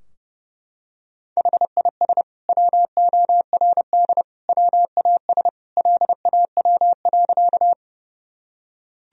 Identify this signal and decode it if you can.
Morse code, heard as a keyed tone, about 30 wpm, 700 Hz. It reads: HIS WORD WAS LAW.
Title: Sentences from Top 1000 Words - Review Part 3 - 30wpm